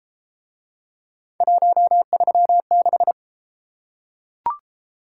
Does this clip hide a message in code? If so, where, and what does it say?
Yes: Morse code from 1.4 to 3.1 s, heard as a keyed tone, about 28 wpm, 700 Hz. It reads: 136